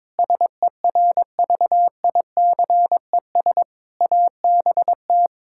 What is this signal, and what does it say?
Morse code, heard as a keyed tone, about 22 wpm, 700 Hz. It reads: SERVICES ABT
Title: Practice QSO - Exchange 3 - More Info - 22wpm